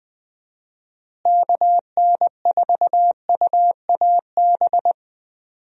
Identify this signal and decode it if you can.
Morse code, heard as a keyed tone, about 20 wpm, 700 Hz. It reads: KN4UAB
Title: USA Call Signs - 6 Characters - Repeated 2x - 20wpm